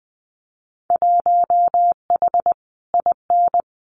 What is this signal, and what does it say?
Morse code, heard as a keyed tone, about 19 wpm, 700 Hz. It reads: 1H IN
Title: Winter Field Day Exchanges 20wpm